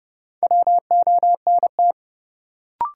Morse code, heard as a keyed tone, about 31 wpm, 700 Hz. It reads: WONT
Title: Sentences from Top 1000 Words - Review Part 6 - New Word Drill - 30wpm